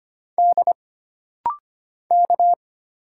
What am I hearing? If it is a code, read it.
Morse code, heard as a keyed tone, about 25 wpm, 700 Hz. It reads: D K
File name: Single Character (Letter:Number:Punctuation:Prosign) - Rapid-Fire - 25wpm